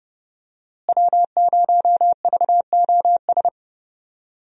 Morse code, heard as a keyed tone, about 31 wpm, 700 Hz. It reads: W0VOS
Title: USA Call Signs - 5 Characters - Repeated 3x - 30wpm